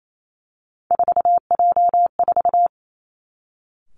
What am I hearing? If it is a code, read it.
Morse code, heard as a keyed tone, about 28 wpm, 700 Hz. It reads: 4J4
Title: Three Challenging Letter-Number Combinations 28wpm